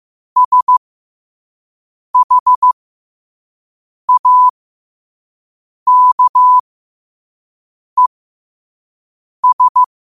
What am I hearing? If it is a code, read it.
Morse code, heard as a keyed tone, about 15 wpm, spaced as for 5 wpm, 1000 Hz. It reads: SHAKES
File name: qotdfiles / 5wpm